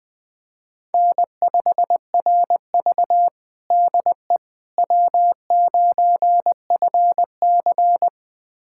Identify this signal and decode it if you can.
Morse code, heard as a keyed tone, about 20 wpm, 700 Hz. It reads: N5RV DE W9FC
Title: USA Callsign DE Callsign - 4 Characters - 20wpm